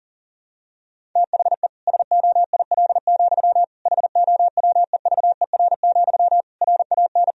Morse code, heard as a keyed tone, about 37 wpm, 700 Hz. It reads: THE SOIL, HOWEVER, RAN
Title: Sentences from Top 1000 Words - Review Part 4 - 40wpm